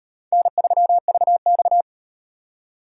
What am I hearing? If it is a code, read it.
Morse code, heard as a keyed tone, about 38 wpm, 700 Hz. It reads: N3VX